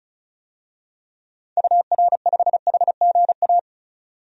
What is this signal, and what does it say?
Morse code, heard as a keyed tone, about 34 wpm, 700 Hz. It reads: UR5HGA
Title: DXCC-100 Call Signs - 6 Characters - 35wpm